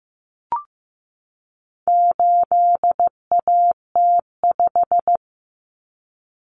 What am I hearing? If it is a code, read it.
Morse code, heard as a keyed tone, about 15 wpm, 700 Hz. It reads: 8AT5
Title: Common Cut Numbers - 4-Digit 15wpm